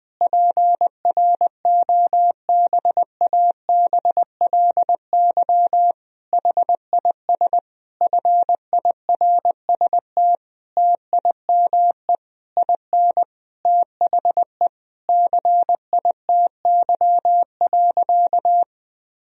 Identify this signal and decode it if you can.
Morse code, heard as a keyed tone, about 20 wpm, 700 Hz. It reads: PROBABLY HIS FIRST TIME IN THE CITY.
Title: Sentences from Top 1000 Words - Review Part 4 - 20wpm